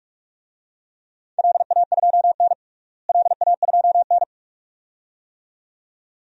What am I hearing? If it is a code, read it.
Morse code, heard as a keyed tone, about 46 wpm, 700 Hz. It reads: LA2N LA2N
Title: DXCC-100 Call Signs - 4 Characters - Repeated 2x - 45wpm